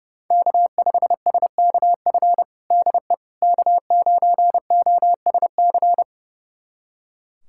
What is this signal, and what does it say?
Morse code, heard as a keyed tone, about 30 wpm, 700 Hz. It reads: K5SKF DE K9OSC
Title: USA Callsign DE Callsign - 5 Characters - 30wpm